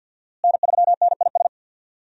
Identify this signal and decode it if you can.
Morse code, heard as a keyed tone, about 46 wpm, 700 Hz. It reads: N3NIS
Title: USA Call Signs - 5 Characters - Repeated 3x - 50wpm